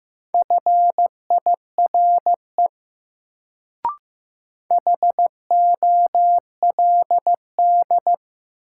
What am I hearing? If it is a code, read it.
Morse code, heard as a keyed tone, about 15 wpm, 700 Hz. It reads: FIRE HOLD
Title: Sentences from Top 400 Words - New Word Drill - 15wpm